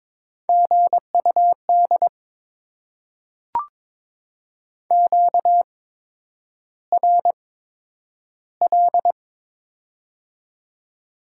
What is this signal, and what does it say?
Morse code, heard as a keyed tone, about 22 wpm, spaced as for 9 wpm, 700 Hz. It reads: GUD QRL
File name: Top 20 QSO Elements - 8x Character Spacing - 22wpm